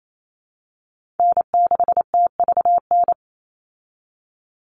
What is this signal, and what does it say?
Morse code, heard as a keyed tone, about 28 wpm, 700 Hz. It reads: N6TVN
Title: Call Signs Repeated 2x - 28wpm